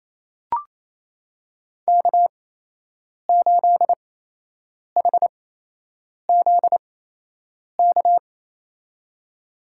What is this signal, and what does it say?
Morse code, heard as a keyed tone, about 28 wpm, spaced as for 8 wpm, 700 Hz. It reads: K8HZK
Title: USA Call Signs - 5 Characters - 8x Character Spacing - 28wpm